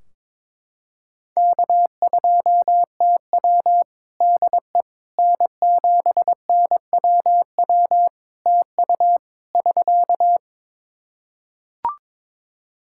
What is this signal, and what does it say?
Morse code, heard as a keyed tone, about 22 wpm, 700 Hz. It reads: K2TW DE N7NWW TU <SK>
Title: K1USN SST QSO Practice 22wpm